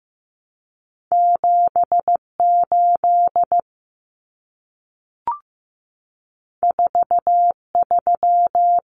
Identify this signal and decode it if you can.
Morse code, heard as a keyed tone, about 15 wpm, 700 Hz. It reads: 78 43
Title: Common Cut Numbers - 2-Digit 15wpm